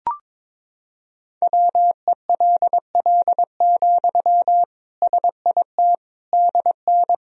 Morse code, heard as a keyed tone, about 22 wpm, 700 Hz. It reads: WELL, SIT DN
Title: Sentences from Top 500 Words and Abbreviations 22wpm